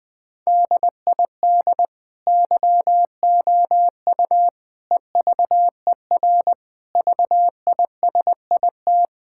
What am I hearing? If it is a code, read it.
Morse code, heard as a keyed tone, about 20 wpm, 700 Hz. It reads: DID YOU EVER VISIT
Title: Sentences from Top 1000 Words - Review Part 6 - 20wpm